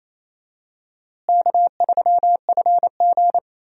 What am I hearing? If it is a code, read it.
Morse code, heard as a keyed tone, about 28 wpm, 700 Hz. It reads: K3FG
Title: USA Call Signs - 4 Characters - Repeated 2x - 28wpm